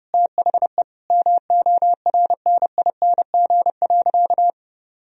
Morse code, heard as a keyed tone, about 30 wpm, 700 Hz. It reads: THE MORNING.
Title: Sentences from Top 1000 Words - Review Part 2 - 30wpm